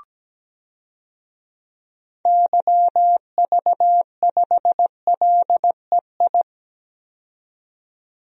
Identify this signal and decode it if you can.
Morse code, heard as a keyed tone, about 17 wpm, 700 Hz. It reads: YV5LEI
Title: DXCC-100 Call Signs - 6 Characters - 17wpm